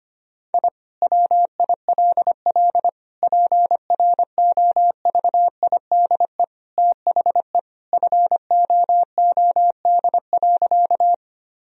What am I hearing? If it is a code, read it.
Morse code, heard as a keyed tone, about 25 wpm, 700 Hz. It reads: I WILL PROVIDE THE FOOD.
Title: Sentences from Top 1000 Words - Review Part 3 - 25wpm